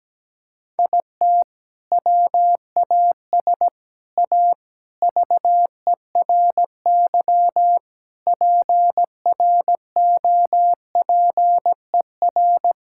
Morse code, heard as a keyed tone, about 17 wpm, 700 Hz. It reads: IT WAS A VERY PROPER
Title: Sentences from Top 1000 Words - Review Part 5 - 17wpm